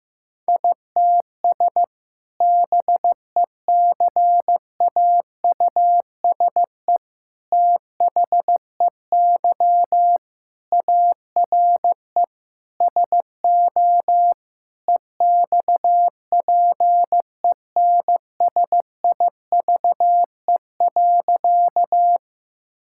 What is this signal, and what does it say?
Morse code, heard as a keyed tone, about 16 wpm, 700 Hz. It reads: ITS BECAUSE THEY ARE SO EXPENSIVE.
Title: Sentences from Top 1000 Words - Review Part 6 - 15wpm